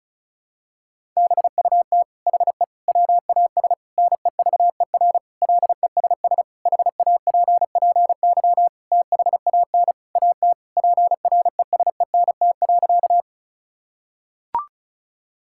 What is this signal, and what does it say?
Morse code, heard as a keyed tone, about 36 wpm, 700 Hz. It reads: BUT HE WAS NEVER LESS HAPPY THAN AT PRESENT.
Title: Sentences from Top 1000 Words - Review Part 5 - 35wpm